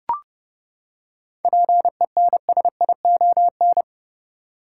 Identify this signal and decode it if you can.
Morse code, heard as a keyed tone, about 32 wpm, 700 Hz. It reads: PENSION
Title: Seven Letter Words - 30wpm